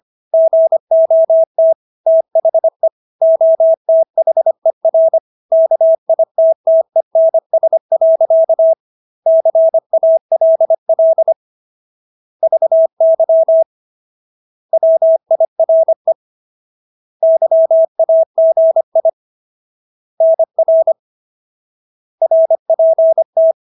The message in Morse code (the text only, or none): GOT THE OTHER KITTENS. CALL VY WIRE YAGI NR RPT